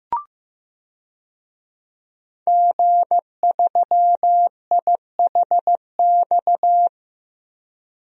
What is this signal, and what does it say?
Morse code, heard as a keyed tone, about 16 wpm, 700 Hz. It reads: G3IHX
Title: DXCC-100 Call Signs - 5 Characters - 15wpm